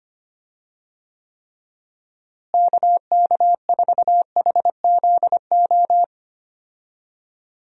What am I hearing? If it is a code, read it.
Morse code, heard as a keyed tone, about 25 wpm, 700 Hz. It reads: KK4HZO
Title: USA Call Signs - 6 Characters - Repeated 2x - 25wpm